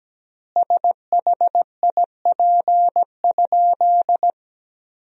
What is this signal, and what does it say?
Morse code, heard as a keyed tone, about 17 wpm, 700 Hz. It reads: SHIP?
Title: 4x Word-Spacing - Sentences from Top 400 Words - 17wpm